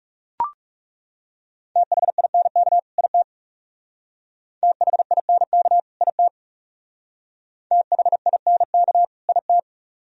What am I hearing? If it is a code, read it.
Morse code, heard as a keyed tone, about 37 wpm, 700 Hz. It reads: THINK IT THINK IT THINK IT
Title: Speed-Racing - Sets of 2 Words 30wpm